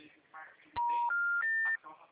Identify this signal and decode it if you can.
Special information tone, with speech.